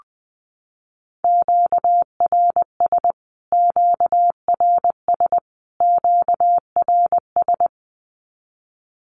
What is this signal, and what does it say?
Morse code, heard as a keyed tone, about 20 wpm, 700 Hz. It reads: QRS QRS QRS
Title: Q-Codes Repeated 3x 20wpm